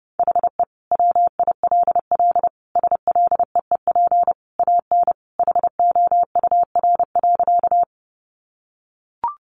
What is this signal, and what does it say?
Morse code, heard as a keyed tone, about 30 wpm, 700 Hz. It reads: HE WILL SLEEP AN HOUR.
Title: Sentences from Top 500 Words 30wpm